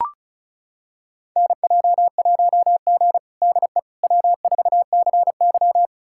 Morse code, heard as a keyed tone, about 36 wpm, 700 Hz. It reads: NJ1G DE W4CY
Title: USA Callsign DE Callsign - 4 Characters - 35wpm